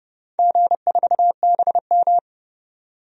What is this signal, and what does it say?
Morse code, heard as a keyed tone, about 30 wpm, 700 Hz. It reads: G4BM
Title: DXCC-100 Call Signs - 4 Characters - 30wpm